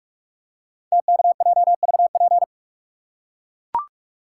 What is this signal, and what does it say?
Morse code, heard as a keyed tone, about 44 wpm, 690 Hz. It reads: TKJVP
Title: Code Groups - 5 Letters - 45wpm